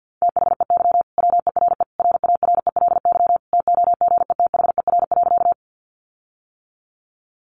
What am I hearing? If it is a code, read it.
Morse code, heard as a keyed tone, about 50 wpm, 700 Hz. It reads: THEY WERE RARELY TOGETHER.
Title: Sentences from Top 1400 Words 50wpm